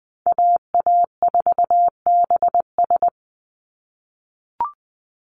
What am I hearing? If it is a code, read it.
Morse code, heard as a keyed tone, about 20 wpm, 700 Hz. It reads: AA4BS
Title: USA Call Signs 20wpm